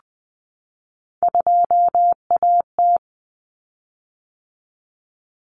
Morse code, heard as a keyed tone, about 20 wpm, 700 Hz. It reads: 2AT
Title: Common Cut Numbers - 3-Digit 20wpm